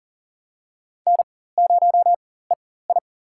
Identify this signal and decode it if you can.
Morse code, heard as a keyed tone, about 40 wpm, spaced as for 19 wpm, 700 Hz. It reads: N0EI
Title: USA Call Signs - 4 Characters - 4x Character Spacing - 40wpm